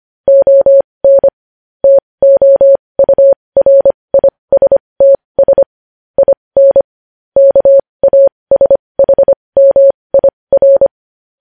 Morse code, heard as a keyed tone, about 25 wpm, spaced as for 20 wpm, 550 Hz. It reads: ON TOURISTS IN KASHMIR
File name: podcast / News headlines-1x-25x20